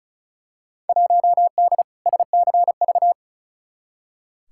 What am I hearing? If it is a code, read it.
Morse code, heard as a keyed tone, about 36 wpm, 700 Hz. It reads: 1D SCV
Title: Field Day Exchanges Encore - 35wpm